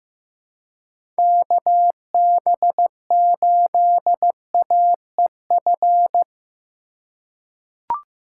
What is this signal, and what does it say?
Morse code, heard as a keyed tone, about 15 wpm, 700 Hz. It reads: KB8AEF
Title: USA Call Signs - 3x Character Spacing - 15wpm